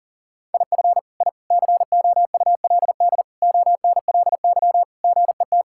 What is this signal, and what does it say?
Morse code, heard as a keyed tone, about 40 wpm, 700 Hz. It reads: IF I COULD ONLY GET
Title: Sentences from Top 1000 Words - Review Part 6 - 40wpm